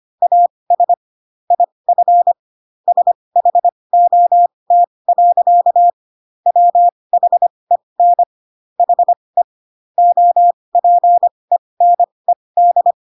Morse code, heard as a keyed tone, about 25 wpm, spaced as for 20 wpm, 700 Hz. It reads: AS IF SHOT. WHEN HE OPENED
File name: Pinocchio / Pinocchio ch1 15 25 ik1hge